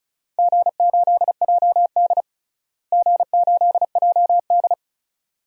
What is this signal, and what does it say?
Morse code, heard as a keyed tone, about 34 wpm, 700 Hz. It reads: G8JD G8JD
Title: DXCC-100 Call Signs - 4 Characters - Repeated 2x - 35wpm